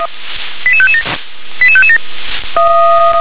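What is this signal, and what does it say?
Five-tone selective call (ZVEI), heard as quick runs of five tones, tones of about 70 ms. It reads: 88557 88557
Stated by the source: ZVEI five-tone selective call received by SDR.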